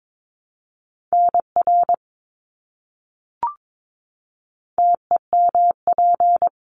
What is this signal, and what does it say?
Morse code, heard as a keyed tone, about 22 wpm, 700 Hz. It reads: NR TEMP
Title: Abbreviations Repeated 3x - 22wpm